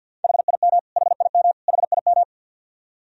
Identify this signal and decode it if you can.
Morse code, heard as a keyed tone, about 52 wpm, 700 Hz. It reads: HIM HIM HIM
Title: Top 100 Words Repeated 50wpm